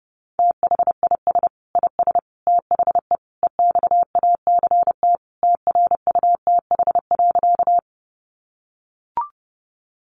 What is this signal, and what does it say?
Morse code, heard as a keyed tone, about 28 wpm, 700 Hz. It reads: THIS IS THE EXACT TRUTH.
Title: Sentences from Top 1300 Words 30wpm